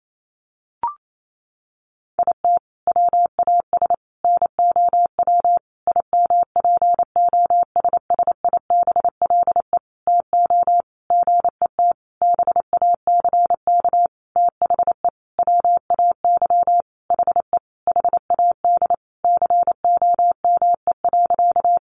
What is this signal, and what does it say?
Morse code, heard as a keyed tone, about 28 wpm, 700 Hz. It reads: IT WAS NOW IMPOSSIBLE TO GET BACK THE WAY HE HAD COME.
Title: Sentences from Top 1100 Words 28wpm